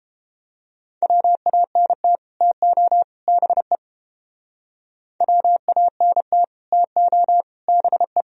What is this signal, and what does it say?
Morse code, heard as a keyed tone, about 31 wpm, 700 Hz. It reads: WANT TO BE WANT TO BE